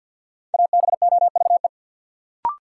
Morse code, heard as a keyed tone, about 47 wpm, 700 Hz. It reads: ABOVE
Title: Sentences from Top 1000 Words - Review Part 2 - New Word Drill - 50wpm